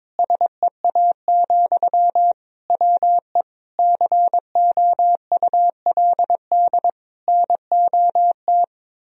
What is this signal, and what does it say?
Morse code, heard as a keyed tone, about 22 wpm, 700 Hz. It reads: SEA, WE COULD NOT
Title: Sentences from Top 1000 Words - Review Part 2 - 22wpm